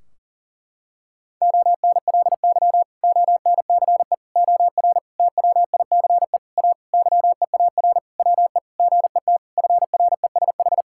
Morse code, heard as a keyed tone, about 40 wpm, 700 Hz. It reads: ONLY ONCE OR TWICE A YEAR WE GET FRESH
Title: Sentences from Top 1600 Words 40wpm